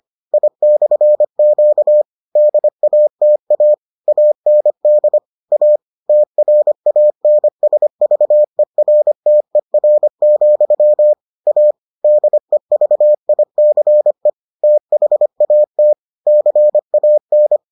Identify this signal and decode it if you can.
Morse code, heard as a keyed tone, about 25 wpm, 600 Hz. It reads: I/Q DATA AND A TRANSVERTER, A DEVICE THAT CAN